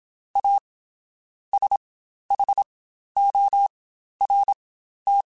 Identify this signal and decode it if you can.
Morse code, heard as a keyed tone, about 27 wpm, spaced as for 12 wpm, 800 Hz. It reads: A SHORT
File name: Quotes Spaced / Quotes Spaced-mp3s / Temper sp @25